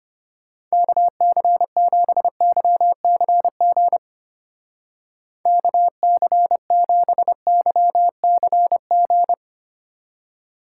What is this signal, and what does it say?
Morse code, heard as a keyed tone, about 29 wpm, 700 Hz. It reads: KC7YCG KC7YCG
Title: Speed Racing - USA 2x3 Call Signs 25wpm